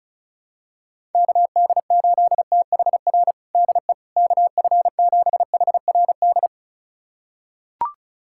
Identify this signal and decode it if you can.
Morse code, heard as a keyed tone, about 34 wpm, 700 Hz. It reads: KD8THR DE KF7HRD
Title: USA Callsign DE Callsign - 6 Characters - 35wpm